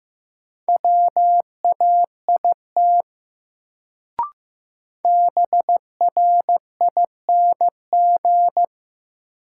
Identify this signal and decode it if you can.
Morse code, heard as a keyed tone, about 15 wpm, 700 Hz. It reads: WAIT BRING
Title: Sentences from Top 500 Words - New Word Drill - 15wpm